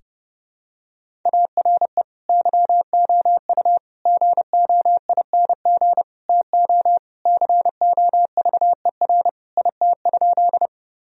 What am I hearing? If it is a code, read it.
Morse code, heard as a keyed tone, about 30 wpm, 700 Hz. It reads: ARE YOU GOING TO COVER IT?
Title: Sentences from Top 1000 Words - Review Part 1 - 30wpm